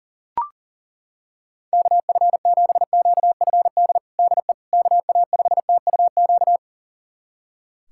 Morse code, heard as a keyed tone, about 40 wpm, 700 Hz. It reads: KF7QFD DE KA5TUQ
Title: USA Callsign DE Callsign - 6 Characters - 40wpm